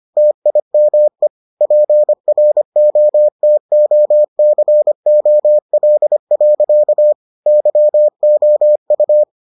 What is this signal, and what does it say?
Morse code, heard as a keyed tone, about 25 wpm, 600 Hz. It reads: TIME PROTOCOL. YOU